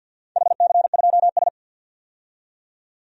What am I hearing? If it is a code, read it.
Morse code, heard as a keyed tone, about 53 wpm, 700 Hz. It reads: HC2S
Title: DXCC-100 Call Signs - 4 Characters - 50wpm